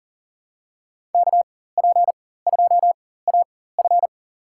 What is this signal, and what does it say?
Morse code, heard as a keyed tone, about 40 wpm, spaced as for 19 wpm, 700 Hz. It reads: KP2AF
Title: USA Call Signs - 5 Characters - 4x Character Spacing - 40wpm